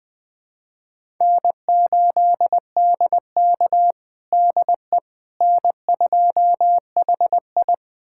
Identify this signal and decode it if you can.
Morse code, heard as a keyed tone, about 21 wpm, 700 Hz. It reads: N8DK DE N2HI
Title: USA Callsign DE Callsign - 4 Characters - 20wpm